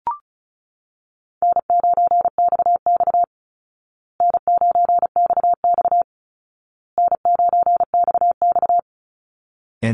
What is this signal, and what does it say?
Morse code, heard as a keyed tone, about 34 wpm, 700 Hz. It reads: N9XX N9XX N9XX
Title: Call Signs Repeated 2x - 35wpm